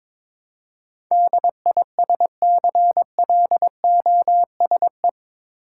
Morse code, heard as a keyed tone, about 22 wpm, 700 Hz. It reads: DISCLOSE
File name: practice / 22wpm / Eight Letter Words - 22wpm